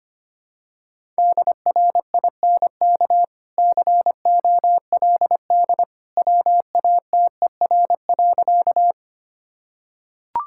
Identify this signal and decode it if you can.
Morse code, heard as a keyed tone, about 25 wpm, 700 Hz. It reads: DRINK COLD WATER.
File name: Easy Sentences with Hints - 25wpm